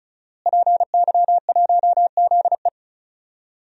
Morse code, heard as a keyed tone, about 36 wpm, 700 Hz. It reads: PY1ZE
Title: DXCC-100 Call Signs - 5 Characters - Repeated 3x - 35wpm